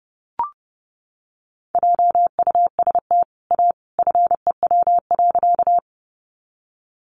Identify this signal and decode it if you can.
Morse code, heard as a keyed tone, about 30 wpm, 700 Hz. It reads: JUST A FEW.
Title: Sentences from Top 300 Words 30wpm